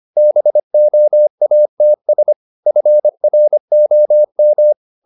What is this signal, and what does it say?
Morse code, heard as a keyed tone, about 25 wpm, 600 Hz. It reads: BOATS FROM
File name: the old man and the sea part 1 25wpm